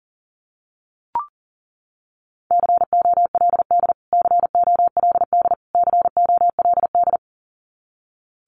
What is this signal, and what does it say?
Morse code, heard as a keyed tone, about 40 wpm, 700 Hz. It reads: COLD COLD COLD
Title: Top 500 Words Repeated 40wpm